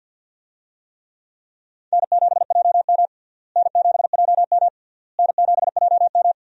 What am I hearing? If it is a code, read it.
Morse code, heard as a keyed tone, about 52 wpm, 700 Hz. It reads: N7JM N7JM N7JM
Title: USA Call Signs - 4 Characters - Repeated 3x - 50wpm